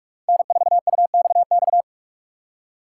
Morse code, heard as a keyed tone, about 44 wpm, 700 Hz. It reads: N4UXX
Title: USA Call Signs - 5 Characters - 6x Character Spacing - 45wpm